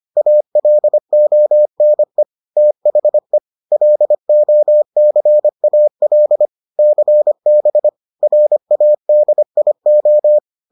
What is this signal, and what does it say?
Morse code, heard as a keyed tone, about 25 wpm, 600 Hz. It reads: ALONE THE LOCAL CB RADIO